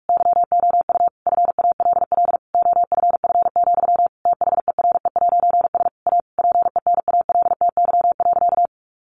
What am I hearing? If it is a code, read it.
Morse code, heard as a keyed tone, about 44 wpm, 700 Hz. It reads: YOU FALL OFF, THERE'S A PENALTY.
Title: A Sound Of Thunder 45wpm